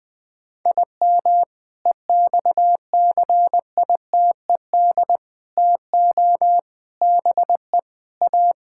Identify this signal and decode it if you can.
Morse code, heard as a keyed tone, about 20 wpm, 700 Hz. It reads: IM EXCITED TO BE A
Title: Sentences with 5-Word Phrases - 20wpm